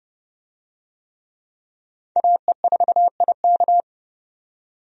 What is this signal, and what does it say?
Morse code, heard as a keyed tone, about 30 wpm, 700 Hz. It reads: AE4IK
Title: USA Call Signs - 5 Characters - 30wpm